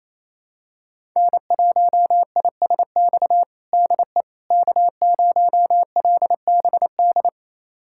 Morse code, heard as a keyed tone, about 28 wpm, 700 Hz. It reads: N1ISX DE K0LBD
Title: USA Callsign DE Callsign - 5 Characters - 28wpm